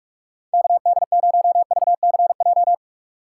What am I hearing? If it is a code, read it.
Morse code, heard as a keyed tone, about 57 wpm, 700 Hz. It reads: KD0VCJ